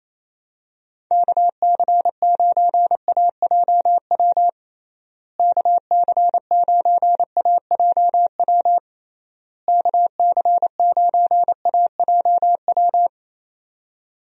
Morse code, heard as a keyed tone, about 28 wpm, 700 Hz. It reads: KC9AJW KC9AJW KC9AJW
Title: USA Call Signs - 6 Characters - Repeated 3x - 28wpm